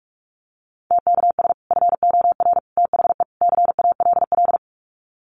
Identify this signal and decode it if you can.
Morse code, heard as a keyed tone, about 44 wpm, 700 Hz. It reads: TKS FOR THE CALL
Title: Common QSO Phrases 45wpm